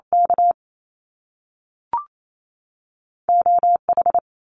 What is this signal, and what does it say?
Morse code, heard as a keyed tone, about 28 wpm, 700 Hz. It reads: K OH
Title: Top Two Letter Combinations in Words 28wpm